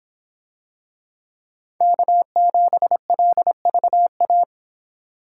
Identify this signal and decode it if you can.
Morse code, heard as a keyed tone, about 26 wpm, 700 Hz. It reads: K7LVA